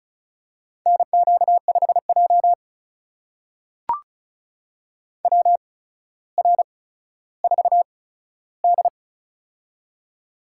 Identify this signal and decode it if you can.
Morse code, heard as a keyed tone, about 36 wpm, spaced as for 15 wpm, 700 Hz. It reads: NQ5J WR4D
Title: USA Call Signs - 4 Characters - 8x Character Spacing - 35wpm